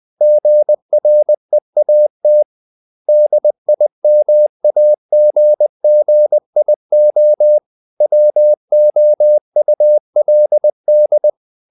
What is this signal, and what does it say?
Morse code, heard as a keyed tone, about 20 wpm, 600 Hz. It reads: GREAT DIMAGGIO WOULD
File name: the old man and the sea part 2 20wpm